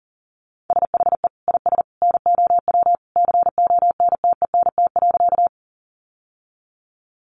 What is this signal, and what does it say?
Morse code, heard as a keyed tone, about 40 wpm, 700 Hz. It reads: SHE IS NOW CONTENT.
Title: Sentences from Top 900 Words 40wpm